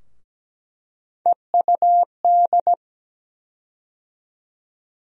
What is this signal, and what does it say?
Morse code, heard as a keyed tone, about 17 wpm, 700 Hz. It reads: EUD
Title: USA Call Sign Postfixes - 3 Letters 17wpm